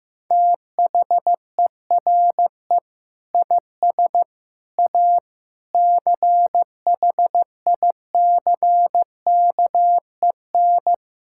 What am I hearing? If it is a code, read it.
Morse code, heard as a keyed tone, about 16 wpm, 700 Hz. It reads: THERE IS A CHICKEN
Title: Sentences from Top 1000 Words - Review Part 5 - 15wpm